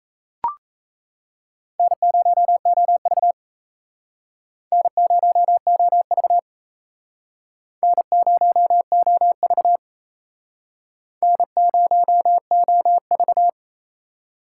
Morse code, heard as a keyed tone, about 36 wpm, 700 Hz. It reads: N0OV N0OV N0OV N0OV